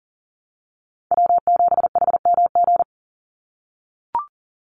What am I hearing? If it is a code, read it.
Morse code, heard as a keyed tone, about 40 wpm, 700 Hz. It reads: W7HMG
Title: USA Call Signs 40wpm